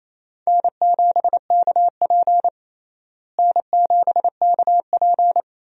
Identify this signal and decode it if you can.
Morse code, heard as a keyed tone, about 28 wpm, 700 Hz. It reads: N7KP N7KP
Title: USA Call Signs - 4 Characters - Repeated 2x - 28wpm